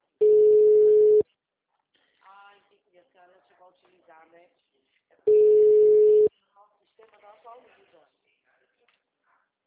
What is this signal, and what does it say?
Ringback tone (Europe, 425 Hz), with speech.
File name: out- -2016- - - .36226